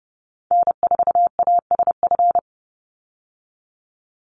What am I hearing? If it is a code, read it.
Morse code, heard as a keyed tone, about 30 wpm, 700 Hz. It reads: N4ASF
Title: Call Signs Repeated 3x - 30wpm